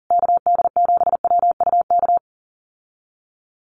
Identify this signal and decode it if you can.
Morse code, heard as a keyed tone, about 40 wpm, 700 Hz. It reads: KD7WUK
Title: USA Call Signs Repeated 3x 40wpm